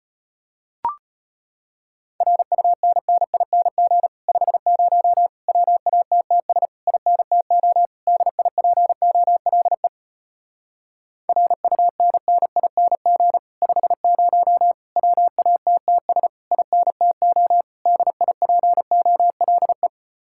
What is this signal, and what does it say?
Morse code, heard as a keyed tone, about 33 wpm, 700 Hz. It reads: RUNNING 50 WATTS INTO DIPOLE RUNNING 50 WATTS INTO DIPOLE